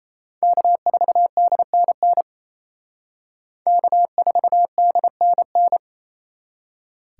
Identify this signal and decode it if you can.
Morse code, heard as a keyed tone, about 35 wpm, 700 Hz. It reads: K4DNN K4DNN